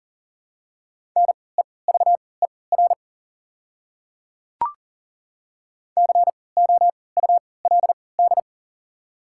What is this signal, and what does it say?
Morse code, heard as a keyed tone, about 40 wpm, spaced as for 23 wpm, 700 Hz. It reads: NEVER COULD
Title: 3x Character Spacing - Top 300 Words - 40wpm